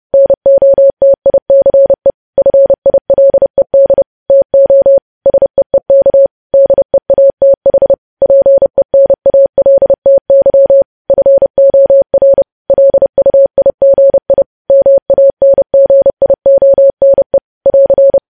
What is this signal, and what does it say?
Morse code, heard as a keyed tone, about 30 wpm, 550 Hz. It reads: NOTICE FILED TO SEEK DEATH PENALTY FOR LUIGI MANGIONE <AR>
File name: podcast / News headlines-1x-30x30